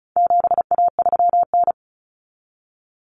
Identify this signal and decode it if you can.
Morse code, heard as a keyed tone, about 34 wpm, 700 Hz. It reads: 7A3N
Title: Common Cut Numbers - 4-Digit 35wpm